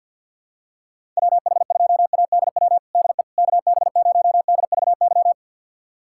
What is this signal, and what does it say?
Morse code, heard as a keyed tone, about 46 wpm, 700 Hz. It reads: WH2ADW DE KB0DVY
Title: USA Callsign DE Callsign - 6 Characters - 50wpm